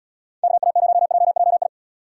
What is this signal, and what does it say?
Morse code, heard as a keyed tone, about 70 wpm, 700 Hz. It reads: KA0JJI